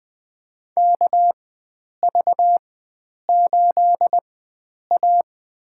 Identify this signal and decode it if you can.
Morse code, heard as a keyed tone, about 20 wpm, spaced as for 9 wpm, 700 Hz. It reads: KV8A
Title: USA Call Signs - 4 Characters - 4x Character Spacing - 20wpm